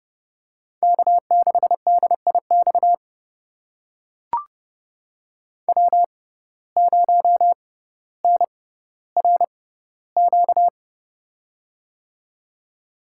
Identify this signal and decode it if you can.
Morse code, heard as a keyed tone, about 30 wpm, spaced as for 15 wpm, 700 Hz. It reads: K6DIX W0NRQ
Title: USA Call Signs - 6x Character Spacing - 30wpm